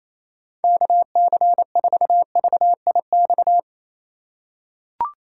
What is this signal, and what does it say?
Morse code, heard as a keyed tone, about 28 wpm, 700 Hz. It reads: KC4VIX
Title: USA Call Signs - 6 Characters - 4x Character Spacing - 28wpm